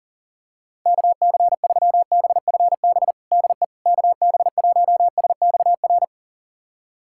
Morse code, heard as a keyed tone, about 40 wpm, 700 Hz. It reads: KC3BFB DE KB1SXR
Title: USA Callsign DE Callsign - 6 Characters - 40wpm